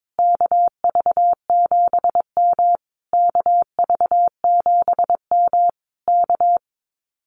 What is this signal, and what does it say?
Morse code, heard as a keyed tone, about 22 wpm, 700 Hz. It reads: KV7M KV7M K
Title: Common QSO Phrases Encore 22wpm